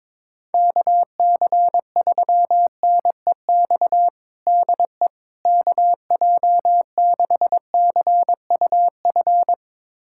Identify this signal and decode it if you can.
Morse code, heard as a keyed tone, about 22 wpm, 700 Hz. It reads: KC3NEX DE KJ6CUF
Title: USA Callsign DE Callsign - 6 Characters - 22wpm